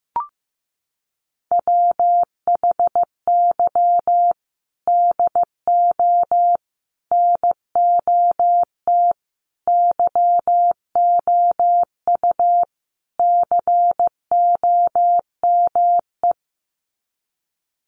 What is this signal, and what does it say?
Morse code, heard as a keyed tone, about 15 wpm, 700 Hz. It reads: WHY DO NOT YOU COME
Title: Sets of 5 Words Encore2 15wpm